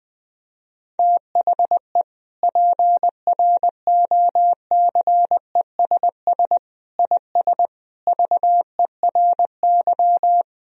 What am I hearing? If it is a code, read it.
Morse code, heard as a keyed tone, about 20 wpm, 700 Hz. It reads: THE PROCESS IS VERY
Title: Sentences from Top 1000 Words - Review Part 3 - 20wpm